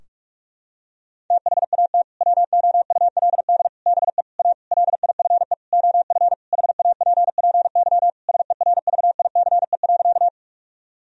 Morse code, heard as a keyed tone, about 46 wpm, 700 Hz. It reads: THAT WOULD BE A LIFE OF HAPPY SERVICE.
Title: Sentences from Top 1000 Words - Review Part 4 - 45wpm